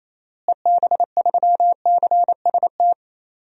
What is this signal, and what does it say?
Morse code, heard as a keyed tone, about 28 wpm, 700 Hz. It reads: EB3CST
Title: DXCC-100 Call Signs - 6 Characters - Repeated 2x - 28wpm